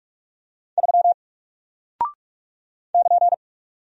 Morse code, heard as a keyed tone, about 44 wpm, 700 Hz. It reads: 3 <KN>
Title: Single Character (Letter:Number:Punctuation:Prosign) 45wpm